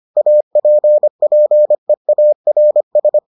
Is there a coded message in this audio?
Yes: Morse code from 0.2 to 3.2 s, heard as a keyed tone, about 25 wpm, 600 Hz. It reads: APPEARS